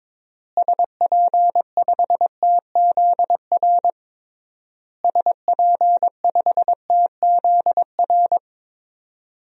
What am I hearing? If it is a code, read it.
Morse code, heard as a keyed tone, about 22 wpm, 700 Hz. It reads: SP5TZR SP5TZR
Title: DXCC-100 Call Signs - 6 Characters - Repeated 3x - 22wpm